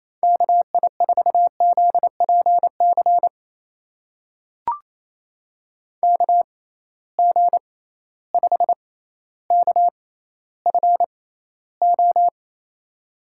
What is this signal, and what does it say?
Morse code, heard as a keyed tone, about 28 wpm, spaced as for 14 wpm, 700 Hz. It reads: KI4ZPC KG5KFO
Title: USA Call Signs - 6 Characters - 6x Character Spacing - 28wpm